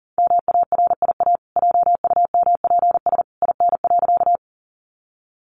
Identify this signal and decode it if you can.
Morse code, heard as a keyed tone, about 40 wpm, 700 Hz. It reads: MARIA JUMPS IN.
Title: Easy Phrases 40wpm